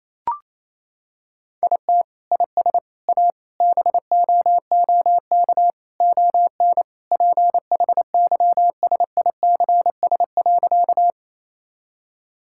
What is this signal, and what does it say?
Morse code, heard as a keyed tone, about 28 wpm, 700 Hz. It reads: IT IS A BOOK ON PHYSICS.
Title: Sentences from Top 1000 Words - Review Part 5 - 28wpm